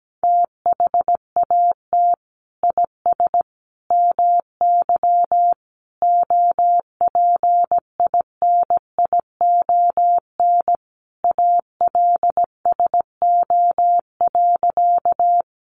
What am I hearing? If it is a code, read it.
Morse code, heard as a keyed tone, about 17 wpm, 700 Hz. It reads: THAT IS MY OPINION ALSO.
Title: Sentences from Top 1200 Words 17wpm